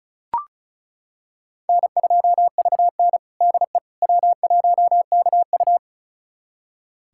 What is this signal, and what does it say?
Morse code, heard as a keyed tone, about 36 wpm, 700 Hz. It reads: N2VN DE W1KU
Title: USA Callsign DE Callsign 35wpm